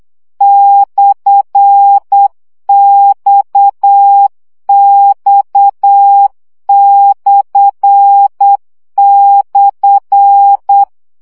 Morse code, heard as a keyed tone, about 8 wpm, 800 Hz. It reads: /XX//
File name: CW gyakorlas x per 7 baud